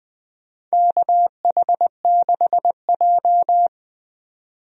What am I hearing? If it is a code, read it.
Morse code, heard as a keyed tone, about 20 wpm, 700 Hz. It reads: KH6J
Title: DXCC-100 Call Signs - 4 Characters - 20wpm